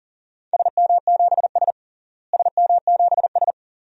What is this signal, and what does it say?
Morse code, heard as a keyed tone, about 37 wpm, 700 Hz. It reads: SM7S SM7S
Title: DXCC-100 Call Signs - 4 Characters - Repeated 2x - 40wpm